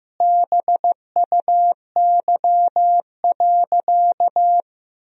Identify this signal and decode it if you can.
Morse code, heard as a keyed tone, about 15 wpm, 700 Hz. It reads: BUY.
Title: Sentences from Top 1000 Words - Review Part 4 - 15wpm